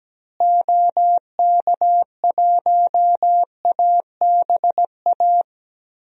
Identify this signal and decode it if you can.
Morse code, heard as a keyed tone, about 17 wpm, 700 Hz. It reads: OK1ABA
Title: DXCC-100 Call Signs - 6 Characters - Repeated 3x - 17wpm